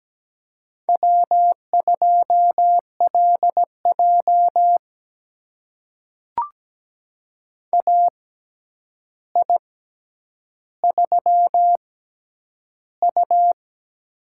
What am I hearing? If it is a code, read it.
Morse code, heard as a keyed tone, about 17 wpm, spaced as for 9 wpm, 700 Hz. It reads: W2LJ AI3U
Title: USA Call Signs - 4 Characters - 6x Character Spacing - 17wpm